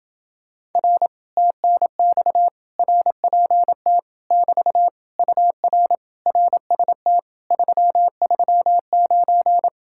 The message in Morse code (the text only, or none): R TNX RPT <BT> UR RST 339